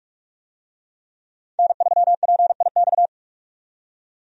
Morse code, heard as a keyed tone, about 46 wpm, 700 Hz. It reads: N3PIX
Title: USA Call Signs - 5 Characters - 45wpm